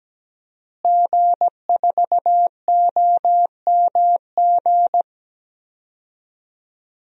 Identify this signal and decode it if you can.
Morse code, heard as a keyed tone, about 17 wpm, 700 Hz. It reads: G4OMG
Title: DXCC-100 Call Signs - 5 Characters - Repeated 3x - 17wpm